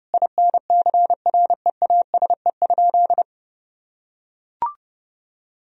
Morse code, heard as a keyed tone, about 30 wpm, 700 Hz. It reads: INCREASE?
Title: Sentences from Top 1000 Words - Review Part 4 - 30wpm